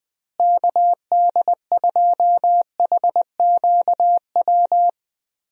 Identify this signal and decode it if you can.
Morse code, heard as a keyed tone, about 20 wpm, 700 Hz. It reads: KD2HQW
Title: USA Call Signs - 6 Characters - Repeated 2x - 20wpm